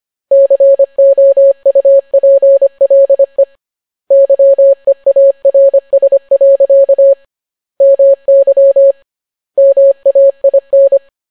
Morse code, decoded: COUPLE YEARS. MY MAIN